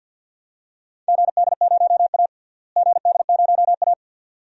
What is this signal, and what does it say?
Morse code, heard as a keyed tone, about 51 wpm, 700 Hz. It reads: GD0A GD0A
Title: DXCC-100 Call Signs - 4 Characters - Repeated 2x - 50wpm